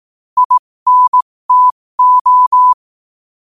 Morse code, heard as a keyed tone, about 18 wpm, spaced as for 15 wpm, 1000 Hz. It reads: INTO
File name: qotdfiles / 15wpm